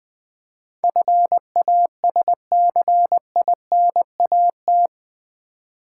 Morse code, heard as a keyed tone, about 20 wpm, 700 Hz. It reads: FASCINAT
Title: Build A Word - 4 Letters At A Time - 11-Letter Words - 20wpm